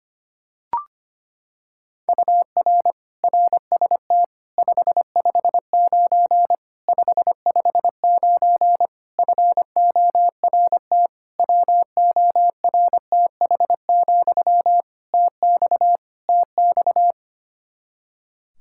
Morse code, heard as a keyed tone, about 25 wpm, 700 Hz. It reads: UR RST 559 559 FORT WORTH, TX TX
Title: Practice QSO - Exchange 1 - Beginning - 25wpm